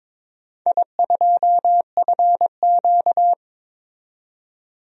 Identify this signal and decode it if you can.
Morse code, heard as a keyed tone, about 22 wpm, 700 Hz. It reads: I2FQ
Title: DXCC-100 Call Signs - 4 Characters - 22wpm